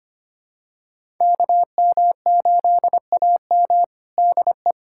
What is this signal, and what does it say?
Morse code, heard as a keyed tone, about 25 wpm, 700 Hz. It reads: KM8AM DE
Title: USA Callsign DE Callsign - 5 Characters - 25wpm